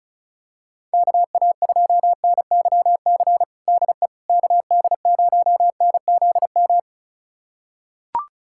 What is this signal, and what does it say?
Morse code, heard as a keyed tone, about 36 wpm, 700 Hz. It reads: KA2NYC DE KD0NZM
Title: USA Callsign DE Callsign - 6 Characters - 35wpm